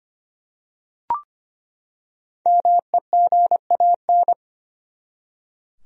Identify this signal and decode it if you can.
Morse code, heard as a keyed tone, about 25 wpm, 700 Hz. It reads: MEGAN
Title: USA Common Names - Max 5 Letters - 25wpm